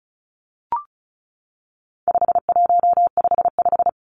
Morse code, heard as a keyed tone, about 34 wpm, 700 Hz. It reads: 5155
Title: Numbers - 4-Digit 35wpm